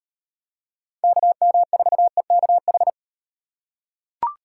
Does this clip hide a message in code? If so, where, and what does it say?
Yes: Morse code from 1.0 to 2.9 s, heard as a keyed tone, about 38 wpm, 700 Hz. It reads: KM4EKH